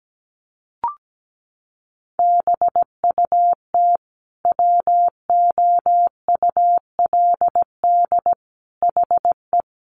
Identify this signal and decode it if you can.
Morse code, heard as a keyed tone, about 17 wpm, 700 Hz. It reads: BUT WOULD HE
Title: Sentences from Top 1400 Words 17wpm